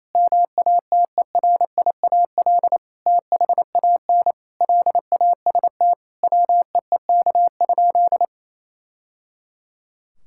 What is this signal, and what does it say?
Morse code, heard as a keyed tone, about 28 wpm, 700 Hz. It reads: MATERIAL THAN LAST WEEK?
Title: Sentences from Top 1000 Words - Review Part 5 - 28wpm